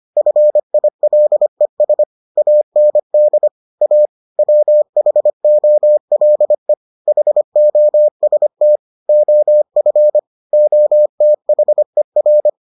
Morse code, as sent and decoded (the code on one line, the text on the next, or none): ..-. .. .-.. ... / .- -. -.. / .- / .-- .... --- .-.. / .... --- ... - / --- ..-. / --- - .... .-.
FILES AND A WHOLE HOST OF OTHER